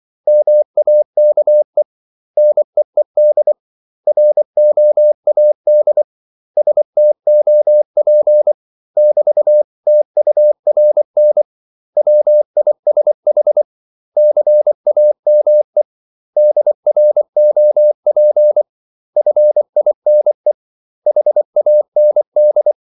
Morse code, heard as a keyed tone, about 24 wpm, 600 Hz. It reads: MAKE NEED ROAD STOP <BT>TURN WISH CAME DROP FINE HAND
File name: CW / Practice / 500MostCommonEnglishWords24